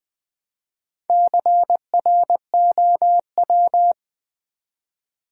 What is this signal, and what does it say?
Morse code, heard as a keyed tone, about 20 wpm, 700 Hz. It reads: CROW
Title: Build A Word - 1 Letter At A Time - 5-Letter Words - 20wpm